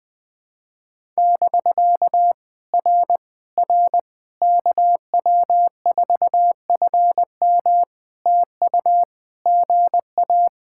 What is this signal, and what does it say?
Morse code, heard as a keyed tone, about 20 wpm, 700 Hz. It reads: <BK> R R KW4FM TU GA
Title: POTA QSO Practice - 20wpm